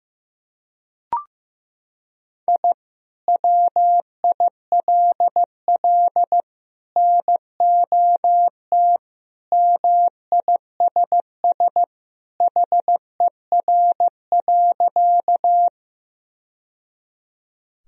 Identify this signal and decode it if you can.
Morse code, heard as a keyed tone, about 15 wpm, 700 Hz. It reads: I WILL NOT MISS HER.
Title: Sentences from Top 1000 Words - Review Part 3 - 15wpm